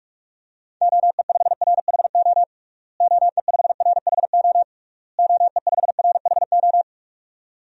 Morse code, heard as a keyed tone, about 46 wpm, 700 Hz. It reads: OE5RHO OE5RHO OE5RHO
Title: DXCC-100 Call Signs - 6 Characters - Repeated 3x - 45wpm